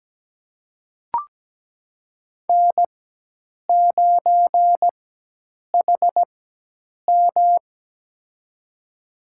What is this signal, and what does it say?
Morse code, heard as a keyed tone, about 17 wpm, spaced as for 8 wpm, 700 Hz. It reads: N9HM
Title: USA Call Signs - 4 Characters - 4x Character Spacing - 17wpm